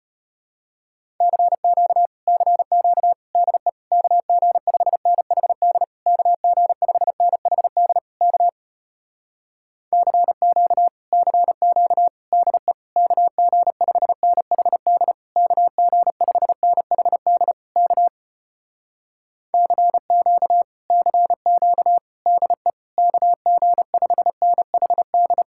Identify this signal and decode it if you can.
Morse code, heard as a keyed tone, about 33 wpm, 700 Hz. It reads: CQ CQ DE KG5NHD KG5NHD K CQ CQ DE KG5NHD KG5NHD K CQ CQ DE KG5NHD